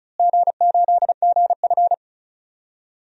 Morse code, heard as a keyed tone, about 34 wpm, 700 Hz. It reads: G8GF
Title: DXCC-100 Call Signs - 4 Characters - Repeated 2x - 35wpm